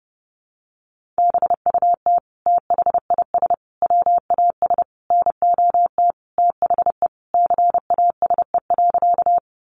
Morse code, heard as a keyed tone, about 30 wpm, 700 Hz. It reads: BUT THIS WAS NOT THE CASE.
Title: Sentences from Top 700 Words 30wpm